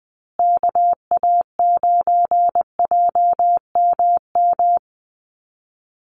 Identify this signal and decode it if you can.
Morse code, heard as a keyed tone, about 20 wpm, 700 Hz. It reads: KA9JMM
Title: USA Call Signs Repeated 2x 20wpm